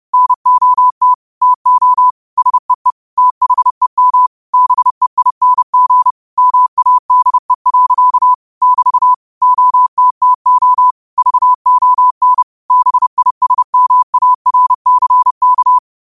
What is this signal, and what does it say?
Morse code, heard as a keyed tone, about 30 wpm, 1000 Hz. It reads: NOT TO SEE THEM BEING MADE. <BT> OTTO VON BISMARCK
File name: qotdfiles / 30wpm